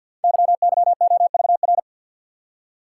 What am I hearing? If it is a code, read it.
Morse code, heard as a keyed tone, about 53 wpm, 700 Hz. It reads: YYOVR
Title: Code Groups - 5 Letters - 50wpm